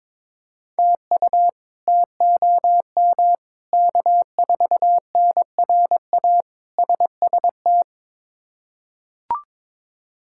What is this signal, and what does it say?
Morse code, heard as a keyed tone, about 22 wpm, 700 Hz. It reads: TU TOM K4NRA SST
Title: K1USN SST QSO Practice - 22wpm